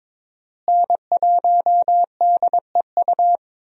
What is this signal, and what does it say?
Morse code, heard as a keyed tone, about 22 wpm, 700 Hz. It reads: N1DEU
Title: USA Call Signs - 5 Characters - Repeated 2x - 22wpm